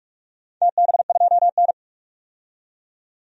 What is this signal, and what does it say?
Morse code, heard as a keyed tone, about 46 wpm, 700 Hz. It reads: TB2N
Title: DXCC-100 Call Signs - 4 Characters - 45wpm